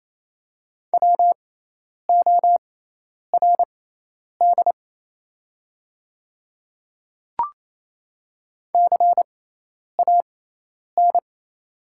Morse code, heard as a keyed tone, about 28 wpm, spaced as for 10 wpm, 700 Hz. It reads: WORD CAN
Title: 6x Character Spacing - Top 300 Words - 28wpm